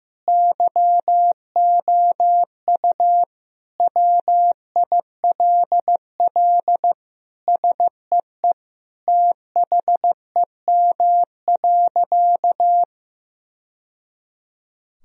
Morse code, heard as a keyed tone, about 16 wpm, 700 Hz. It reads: YOU WILL SEE THEM.
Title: Sentences from Top 1000 Words - Review Part 2 - 15wpm